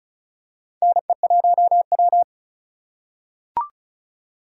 Morse code, heard as a keyed tone, about 34 wpm, 700 Hz. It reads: NE1W
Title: USA Call Signs - 4 Characters - 4x Character Spacing - 35wpm